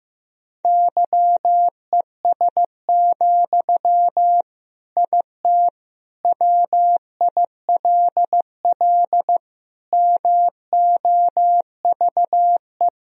Morse code, heard as a keyed tone, about 15 wpm, 700 Hz. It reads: YES, IT WILL MOVE
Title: Sentences from Top 1000 Words - Review Part 1 - 15wpm